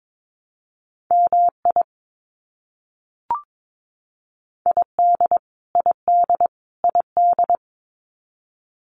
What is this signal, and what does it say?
Morse code, heard as a keyed tone, about 22 wpm, 700 Hz. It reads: MI ID ID ID
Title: US States Repeated 3x 22wpm